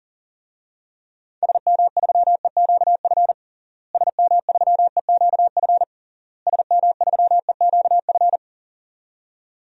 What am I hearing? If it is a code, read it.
Morse code, heard as a keyed tone, about 40 wpm, 700 Hz. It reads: SM3EQF SM3EQF SM3EQF
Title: DXCC-100 Call Signs - 6 Characters - Repeated 3x - 40wpm